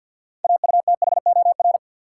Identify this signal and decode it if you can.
Morse code, heard as a keyed tone, about 50 wpm, 700 Hz. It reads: AUTHOR
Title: Sentences from Top 1200 Words - New Word Drill - 50wpm